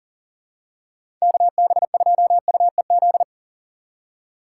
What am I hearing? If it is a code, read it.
Morse code, heard as a keyed tone, about 40 wpm, 700 Hz. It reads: KB2UEZ
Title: USA Call Signs - 6 Characters - 2x Character Spacing - 40wpm